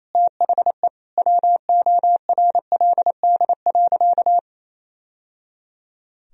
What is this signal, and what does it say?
Morse code, heard as a keyed tone, about 27 wpm, 700 Hz. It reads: THE WORLD.
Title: Sentences from Top 1000 Words - Review Part 4 - 28wpm